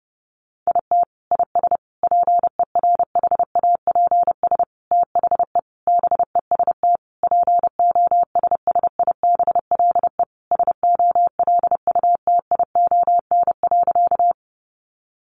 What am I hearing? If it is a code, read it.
Morse code, heard as a keyed tone, about 28 wpm, 700 Hz. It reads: IT IS PERHAPS THE BEST POSSIBLE SOLUTION.
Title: Sentences from Top 1300 Words 30wpm